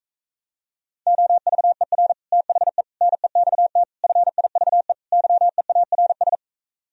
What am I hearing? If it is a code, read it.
Morse code, heard as a keyed tone, about 37 wpm, 700 Hz. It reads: OVER THE NEXT FIVE YEARS